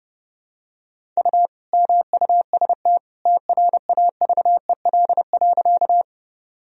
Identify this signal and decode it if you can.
Morse code, heard as a keyed tone, about 30 wpm, 700 Hz. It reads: U MUST TRAVEL.
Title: Sentences from Top 400 Words and Abbreviations 30wpm